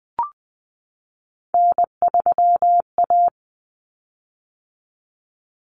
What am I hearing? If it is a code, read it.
Morse code, heard as a keyed tone, about 20 wpm, 700 Hz. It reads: N3A
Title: Common Cut Numbers - 3-Digit 20wpm